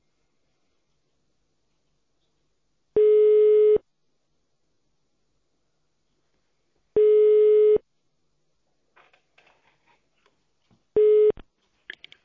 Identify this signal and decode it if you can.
Ringback tone (425 Hz, 1 s on, 3 s off).